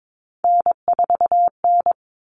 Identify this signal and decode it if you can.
Morse code, heard as a keyed tone, about 22 wpm, 700 Hz. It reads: N4N
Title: All Cut Numbers - 3-Digit 22wpm